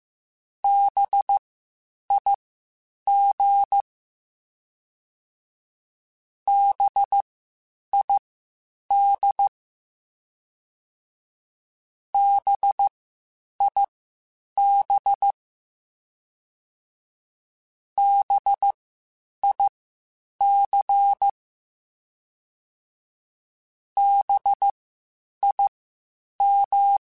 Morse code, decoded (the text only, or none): BIG BID BIB BIC BIM